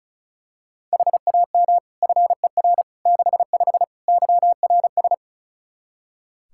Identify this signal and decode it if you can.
Morse code, heard as a keyed tone, about 36 wpm, 700 Hz. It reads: HAM FER 65 YRS
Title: Practice QSO - Exchange 3 - More Info - 35wpm